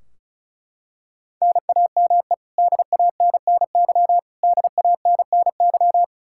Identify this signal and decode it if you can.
Morse code, heard as a keyed tone, about 36 wpm, 700 Hz. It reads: NAME DANNY DANNY
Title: Practice QSO - Exchange 1 - Beginning - 35wpm